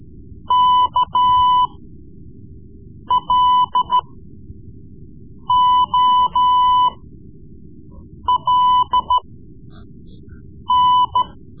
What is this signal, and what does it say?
Morse code, heard as a keyed tone, about 13 wpm, spaced as for 5 wpm, 1000 Hz. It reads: KLOLN
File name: Morse Lessons / Morse Lessons / Tape-3B